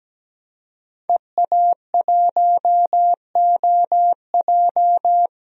Morse code, heard as a keyed tone, about 17 wpm, 700 Hz. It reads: EA1OJ
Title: DXCC-100 Call Signs - 5 Characters - 17wpm